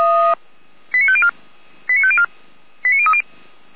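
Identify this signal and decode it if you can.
Five-tone selective call (ZVEI), heard as quick runs of five tones, tones of about 70 ms. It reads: 89504 89504 89030
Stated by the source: ZVEI five-tone selective call received on an SDR.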